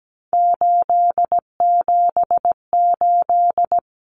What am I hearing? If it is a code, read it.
Morse code, heard as a keyed tone, about 17 wpm, 700 Hz. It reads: 878
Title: Common Cut Numbers - 3-Digit 17wpm